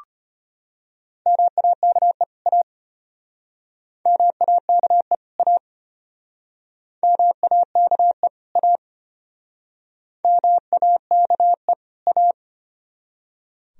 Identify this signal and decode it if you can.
Morse code, heard as a keyed tone, about 33 wpm, 700 Hz. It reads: MAKE A MAKE A MAKE A MAKE A